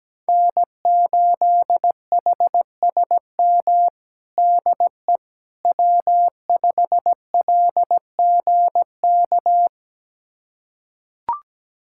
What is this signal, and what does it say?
Morse code, heard as a keyed tone, about 17 wpm, 700 Hz. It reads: N8HSM DE W5LGK
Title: USA Callsign DE Callsign - 5 Characters - 17wpm